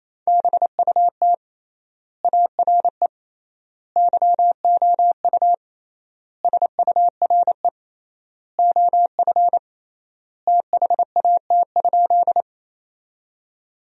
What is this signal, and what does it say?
Morse code, heard as a keyed tone, about 28 wpm, 700 Hz. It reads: BUT ARE YOU SURE OF THAT?
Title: 3x Word-Spacing - Sentences from Top 300 Words - 28wpm